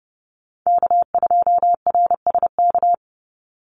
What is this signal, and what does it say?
Morse code, heard as a keyed tone, about 30 wpm, 700 Hz. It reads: K2RSK
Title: Call Signs - 30wpm